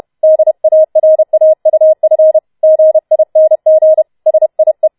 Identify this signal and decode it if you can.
Morse code, heard as a keyed tone, about 31 wpm, 630 Hz. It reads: DARAUF GING SIE